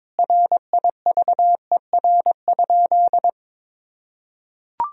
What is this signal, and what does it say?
Morse code, heard as a keyed tone, about 21 wpm, 700 Hz. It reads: RIVER?
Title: Sentences from Top 1000 Words - Review Part 4 - 22wpm